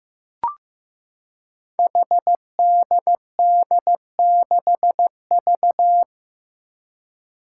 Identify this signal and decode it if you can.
Morse code, heard as a keyed tone, about 15 wpm, 700 Hz. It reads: HDD6V
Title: Five Challenging Letter-Number Combinations 15wpm